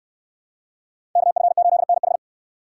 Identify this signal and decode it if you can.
Morse code, heard as a keyed tone, about 69 wpm, 700 Hz. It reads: KF8NH